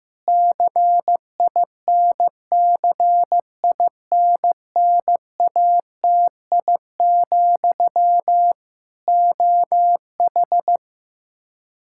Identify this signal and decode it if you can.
Morse code, heard as a keyed tone, about 15 wpm, 700 Hz. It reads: CINCINNATI, OH
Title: K1USN SST QSO Practice 15wpm